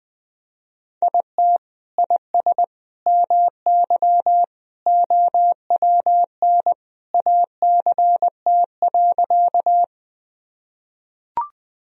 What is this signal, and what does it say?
Morse code, heard as a keyed tone, about 20 wpm, 700 Hz. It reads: IT IS MY OWN ACT.
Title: Sentences from Top 1000 Words - Review Part 1 - 20wpm